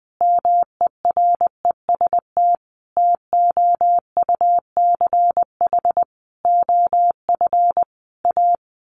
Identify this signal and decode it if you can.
Morse code, heard as a keyed tone, about 20 wpm, 700 Hz. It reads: MEREST TOUCH OF A
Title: A Sound Of Thunder 20wpm